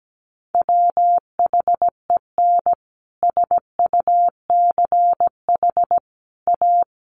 Morse code, heard as a keyed tone, about 17 wpm, 700 Hz. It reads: WHEN SUCH A
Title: Sentences from Top 1400 Words 17wpm